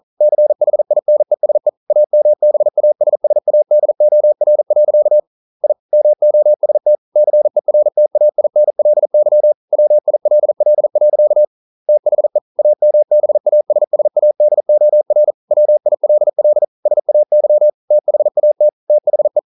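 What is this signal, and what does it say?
Morse code, heard as a keyed tone, about 40 wpm, 600 Hz. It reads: CHINESE AMBASSADOR. I MOST CERTAINLY WILL. THE AMBASSADOR WILL SAY THAT THE
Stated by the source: Morse code practice at 41 wpm, from CW med Gnister.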